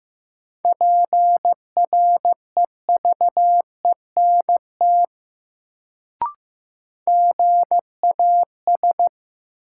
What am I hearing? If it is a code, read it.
Morse code, heard as a keyed tone, about 15 wpm, 700 Hz. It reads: PREVENT GAS
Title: Sentences from Top 1100 Words - New Word Drill Alt - 15wpm